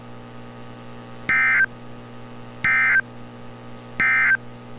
Emergency Alert System (SAME) tones, heard as harsh buzzing bursts. It is heard from 1.2 to 4.4 s.
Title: NOAA Weather Radio - Altoona, PA